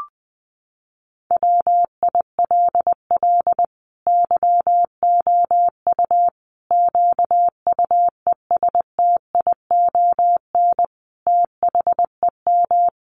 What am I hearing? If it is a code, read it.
Morse code, heard as a keyed tone, about 19 wpm, 700 Hz. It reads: WILL YOU QUESTION THEM
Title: Sentences from Top 1100 Words 20wpm